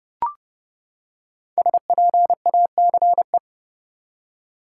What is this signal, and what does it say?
Morse code, heard as a keyed tone, about 31 wpm, 700 Hz. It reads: SPACE
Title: Five Letter Words - 30wpm